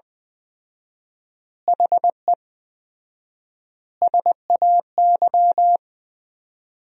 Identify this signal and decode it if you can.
Morse code, heard as a keyed tone, about 20 wpm, 700 Hz. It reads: HE SAY
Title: 4x Word-Spacing - Sentences from Top 300 Words - 20wpm